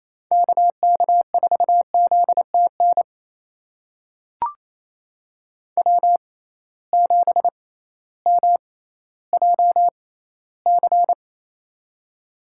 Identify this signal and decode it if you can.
Morse code, heard as a keyed tone, about 28 wpm, spaced as for 15 wpm, 700 Hz. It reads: KK4ZTN W7MJC
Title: USA Call Signs - 6x Character Spacing - 28wpm